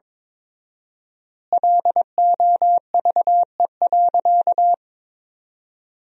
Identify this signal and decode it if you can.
Morse code, heard as a keyed tone, about 22 wpm, 700 Hz. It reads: LOVE.
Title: 4x Word-Spacing - Sentences from Top 400 Words - 22wpm